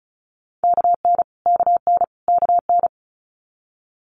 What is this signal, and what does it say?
Morse code, heard as a keyed tone, about 36 wpm, 700 Hz. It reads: KN KN KN
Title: Multi-letter Phonemes Repeated 3x 35wpm